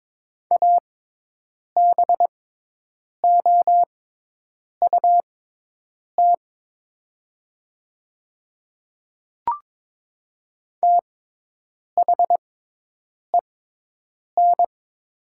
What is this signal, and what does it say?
Morse code, heard as a keyed tone, about 22 wpm, spaced as for 8 wpm, 700 Hz. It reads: ABOUT THEN
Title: 6x Character Spacing - Top 100 Words - 22wpm